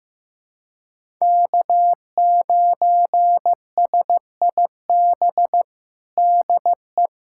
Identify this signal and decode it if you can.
Morse code, heard as a keyed tone, about 15 wpm, 700 Hz. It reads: K9SIB DE
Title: USA Callsign DE Callsign - 5 Characters - 15wpm